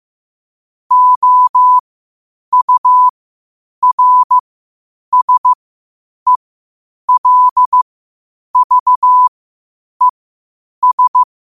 Morse code, heard as a keyed tone, about 15 wpm, spaced as for 8 wpm, 1000 Hz. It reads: OURSELVES
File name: qotdfiles / 8wpm